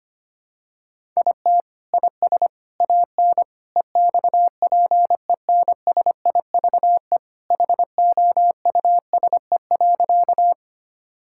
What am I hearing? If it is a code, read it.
Morse code, heard as a keyed tone, about 25 wpm, 700 Hz. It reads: IT IS AN EXPENSIVE HOUSE.
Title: Sentences from Top 1000 Words - Review Part 6 - 25wpm